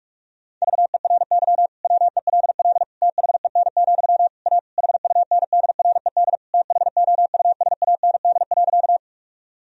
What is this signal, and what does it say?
Morse code, heard as a keyed tone, about 46 wpm, 700 Hz. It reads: VERY WELL THEN, A HUNDRED THOUSAND.
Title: Sentences from Top 1000 Words - Review Part 3 - 45wpm